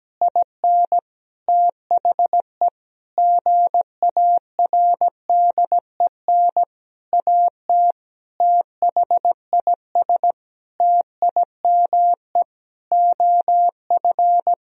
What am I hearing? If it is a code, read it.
Morse code, heard as a keyed tone, about 17 wpm, 700 Hz. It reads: IN THE GARDEN AT THIS TIME OF
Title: Sentences from Top 1000 Words - Review Part 4 - 17wpm